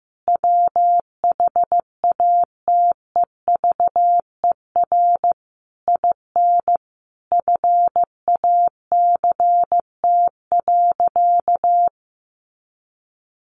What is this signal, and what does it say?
Morse code, heard as a keyed tone, about 15 wpm, 700 Hz. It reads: WHATEVER IN FACT.
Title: Sentences from Top 1200 Words 15wpm